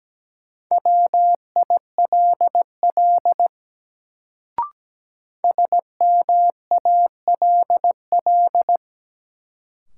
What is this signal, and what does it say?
Morse code, heard as a keyed tone, about 17 wpm, 700 Hz. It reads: WILL SMALL
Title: Sets of 2 Words - New Word Drill - 17wpm